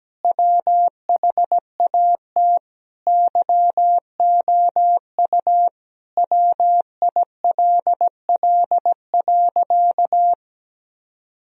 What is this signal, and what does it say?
Morse code, heard as a keyed tone, about 17 wpm, 700 Hz. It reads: WHAT YOU WILL.
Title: Sentences from Top 1000 Words - Review Part 4 - 17wpm